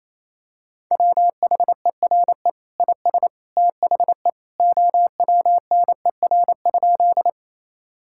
Morse code, heard as a keyed tone, about 28 wpm, 700 Hz. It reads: WHERE IS THE OWNER?
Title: Sentences from Top 1600 Words 28wpm